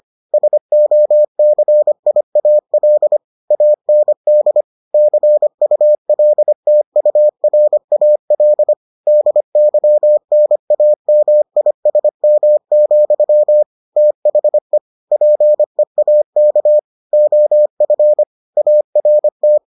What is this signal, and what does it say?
Morse code, heard as a keyed tone, about 25 wpm, 600 Hz. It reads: SOCIAL AND CULTURAL DYNAMISM, THE PEAK OF ART